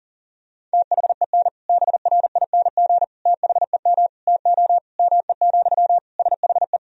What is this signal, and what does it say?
Morse code, heard as a keyed tone, about 37 wpm, 700 Hz. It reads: THEN BRING THEM TO ME, SHE
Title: Sentences from Top 500 Words and Abbreviations 40wpm